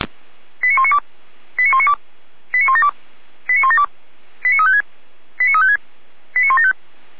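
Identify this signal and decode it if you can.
Five-tone selective call (ZVEI), heard as quick runs of five tones, tones of about 70 ms. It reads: 89192 89192 89172 89172 89356 89356 89175
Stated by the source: ZVEI five-tone selective call received with an SDR.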